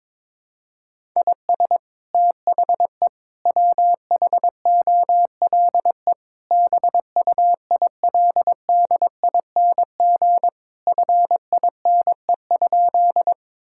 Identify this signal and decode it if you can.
Morse code, heard as a keyed tone, about 22 wpm, 700 Hz. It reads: IS THE WHOLE BUILDING FINE?
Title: Sentences from Top 1000 Words - Review Part 5 - 22wpm